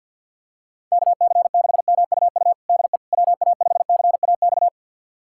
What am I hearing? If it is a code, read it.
Morse code, heard as a keyed tone, about 50 wpm, 700 Hz. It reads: KC6MUU DE WA5CAX
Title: USA Callsign DE Callsign - 6 Characters - 50wpm